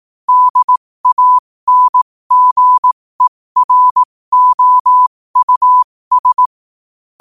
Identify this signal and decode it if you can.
Morse code, heard as a keyed tone, about 18 wpm, spaced as for 15 wpm, 1000 Hz. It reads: DANGEROUS
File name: qotdfiles / 15wpm